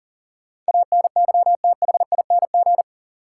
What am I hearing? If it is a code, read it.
Morse code, heard as a keyed tone, about 40 wpm, 700 Hz. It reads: ANYTHING
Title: Sets of 4 Words - New Word Drill - 40wpm